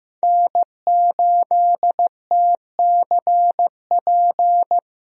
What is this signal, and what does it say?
Morse code, heard as a keyed tone, about 15 wpm, 700 Hz. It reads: N8TCP
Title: USA Call Signs - 5 Characters - 15wpm